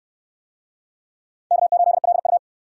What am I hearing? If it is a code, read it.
Morse code, heard as a keyed tone, about 61 wpm, 700 Hz. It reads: K8LU